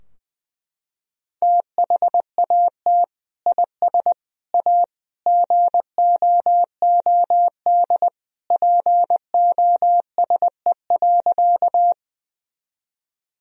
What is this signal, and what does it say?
Morse code, heard as a keyed tone, about 21 wpm, 700 Hz. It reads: THAT IS A GOOD POSE.
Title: Sentences from Top 1000 Words - Review Part 2 - 20wpm